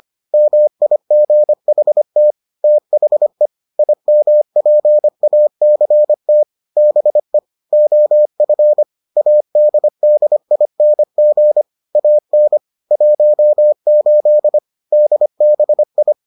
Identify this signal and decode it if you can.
Morse code, heard as a keyed tone, about 25 wpm, 600 Hz. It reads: MIGHT THE IMPACT BE OF ADDING AN 18 DBI